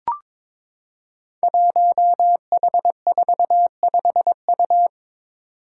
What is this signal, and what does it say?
Morse code, heard as a keyed tone, about 22 wpm, 700 Hz. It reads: 1H45U
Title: Five Challenging Letter-Number Combinations 22wpm